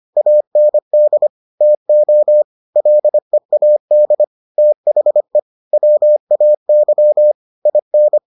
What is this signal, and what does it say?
Morse code, heard as a keyed tone, about 25 wpm, 600 Hz. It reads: AND TO LEAD THE WAY IN